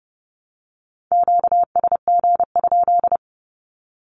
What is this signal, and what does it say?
Morse code, heard as a keyed tone, about 30 wpm, 700 Hz. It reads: QSG?
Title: Q-Codes 30wpm